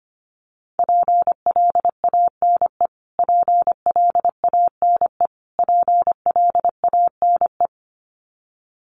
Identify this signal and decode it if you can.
Morse code, heard as a keyed tone, about 25 wpm, 700 Hz. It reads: PLANE PLANE PLANE
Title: Top 500 Words Repeated 25wpm